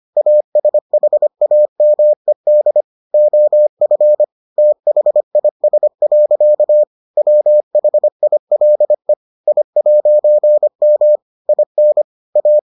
Morse code, heard as a keyed tone, about 25 wpm, 600 Hz. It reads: ASHAMED OF THIS. WHILE I'M IN A